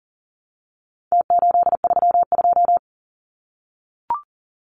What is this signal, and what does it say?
Morse code, heard as a keyed tone, about 40 wpm, 700 Hz. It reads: T832
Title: Numbers - Contest Style - 4-Digit 40wpm